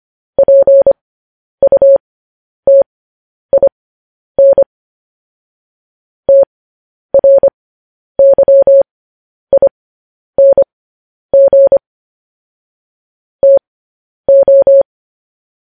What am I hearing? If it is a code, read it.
Morse code, heard as a keyed tone, about 25 wpm, spaced as for 10 wpm, 550 Hz. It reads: PUTIN TRYING TO
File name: podcast / News headlines-1x-25x10